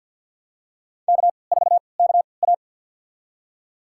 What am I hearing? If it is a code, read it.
Morse code, heard as a keyed tone, about 51 wpm, spaced as for 28 wpm, 700 Hz. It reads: K4KA
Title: USA Call Signs - 4 Characters - 3x Character Spacing - 50wpm